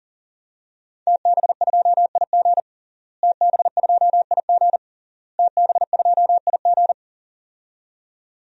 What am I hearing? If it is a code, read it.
Morse code, heard as a keyed tone, about 37 wpm, 700 Hz. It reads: TB2IG TB2IG TB2IG
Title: DXCC-100 Call Signs - 5 Characters - Repeated 3x - 40wpm